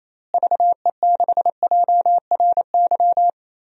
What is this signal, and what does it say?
Morse code, heard as a keyed tone, about 28 wpm, 700 Hz. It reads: VE6JRY
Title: DXCC-100 Call Signs - 6 Characters - Repeated 3x - 28wpm